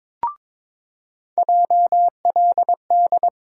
Morse code, heard as a keyed tone, about 22 wpm, 700 Hz. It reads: JLD
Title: USA Call Sign Postfixes - 3 Letters 22wpm